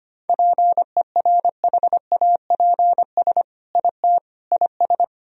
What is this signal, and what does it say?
Morse code, heard as a keyed tone, about 25 wpm, 700 Hz. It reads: PERHAPS IT IS
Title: Sentences from Top 1000 Words - Review Part 2 - 25wpm